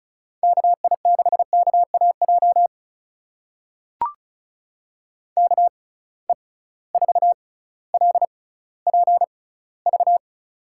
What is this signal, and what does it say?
Morse code, heard as a keyed tone, about 34 wpm, spaced as for 18 wpm, 700 Hz. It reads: KI6KAJ KE4LPV
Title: USA Call Signs - 6 Characters - 6x Character Spacing - 35wpm